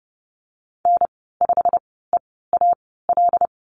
Morse code, heard as a keyed tone, about 30 wpm, spaced as for 17 wpm, 700 Hz. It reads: N5EAL
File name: Lesson 15g - Number 5 - Call Signs 30wpm